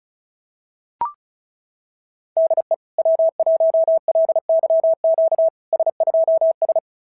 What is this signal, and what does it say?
Morse code, heard as a keyed tone, about 37 wpm, 650 Hz. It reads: DE W1LYQ S2S
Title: SOTA QSO Practice 35wpm